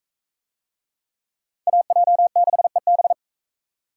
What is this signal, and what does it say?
Morse code, heard as a keyed tone, about 37 wpm, 700 Hz. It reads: AJ6EB